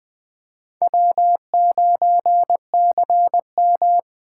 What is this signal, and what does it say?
Morse code, heard as a keyed tone, about 24 wpm, 700 Hz. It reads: W9CM
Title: Speed Racing - USA 1x2 Call Signs 20wpm